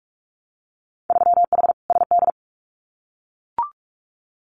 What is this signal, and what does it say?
Morse code, heard as a keyed tone, about 44 wpm, 700 Hz. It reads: 3H SD
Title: Winter Field Day Exchanges Encore 45wpm